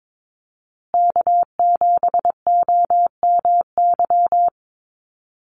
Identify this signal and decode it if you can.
Morse code, heard as a keyed tone, about 22 wpm, 700 Hz. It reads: K7OMY
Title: USA Call Signs Repeated 3x 22wpm